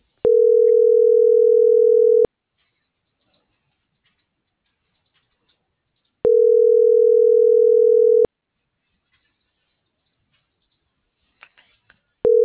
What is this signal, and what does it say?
Ringback tone (North America), 440 + 480 Hz.